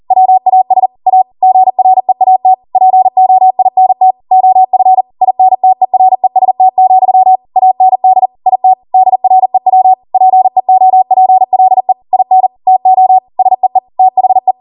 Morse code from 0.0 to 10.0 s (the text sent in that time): WAS A GREAT POINT OF INTEREST, AND IT DREW